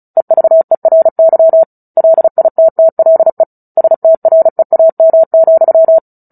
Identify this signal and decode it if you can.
Morse code, heard as a keyed tone, about 34 wpm, 650 Hz. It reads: EVERY LITTLE STREAM,
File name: Book Two / 09-Chapter 9 - 35 WPM